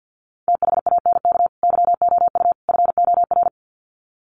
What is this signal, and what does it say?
Morse code, heard as a keyed tone, about 50 wpm, 700 Hz. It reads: THANK YOU FOR
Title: Sets of 3 Words 50wpm